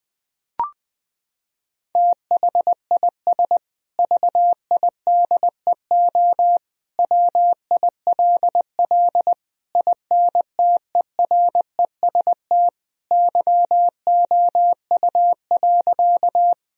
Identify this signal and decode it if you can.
Morse code, heard as a keyed tone, about 20 wpm, 700 Hz. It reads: THIS VIDEO WILL INTEREST YOU.
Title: Sentences from Top 1000 Words - Review Part 4 - 20wpm